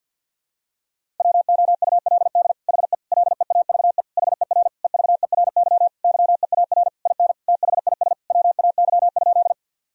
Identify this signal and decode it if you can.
Morse code, heard as a keyed tone, about 47 wpm, 700 Hz. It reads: WOULD HE LEAVE HER EVERY YEAR IN THIS WAY?
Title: Sentences from Top 1000 Words - Review Part 1 - 50wpm